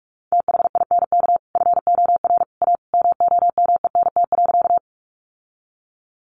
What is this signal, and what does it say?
Morse code, heard as a keyed tone, about 44 wpm, 700 Hz. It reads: THINK FOR A MOMENT.
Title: Sentences from Top 1000 Words 45wpm